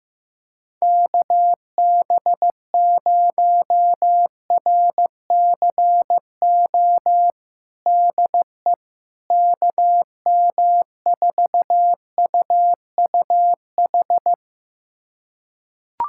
Morse code, heard as a keyed tone, about 16 wpm, 700 Hz. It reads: KB0RCO DE KM4UUH
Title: USA Callsign DE Callsign - 6 Characters - 15wpm